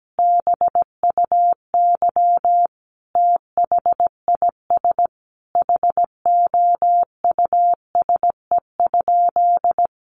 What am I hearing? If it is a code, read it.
Morse code, heard as a keyed tone, about 17 wpm, 700 Hz. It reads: BUY THIS HOUSE?
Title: Sentences from Top 1400 Words 17wpm